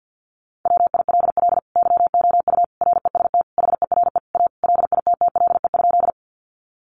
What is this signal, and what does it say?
Morse code, heard as a keyed tone, about 50 wpm, 700 Hz. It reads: WILL YOU REST HERE A LITTLE?
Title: Sentences from Top 500 Words 50wpm